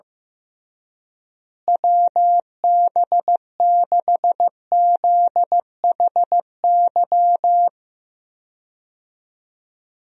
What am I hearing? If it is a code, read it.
Morse code, heard as a keyed tone, about 15 wpm, 700 Hz. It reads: WB6ZHY
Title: USA Call Signs - 6 Characters - Repeated 3x - 15wpm